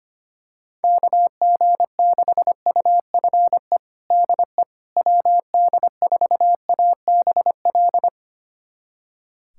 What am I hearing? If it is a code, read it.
Morse code, heard as a keyed tone, about 25 wpm, 700 Hz. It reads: KG6UFE DE WD4ABL
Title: USA Callsign DE Callsign 25wpm